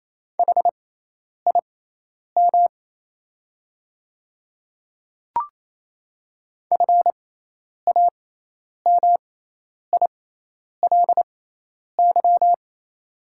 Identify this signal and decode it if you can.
Morse code, heard as a keyed tone, about 28 wpm, spaced as for 10 wpm, 700 Hz. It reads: HIM FAMILY
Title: 6x Character Spacing - Top 500 Words - 28wpm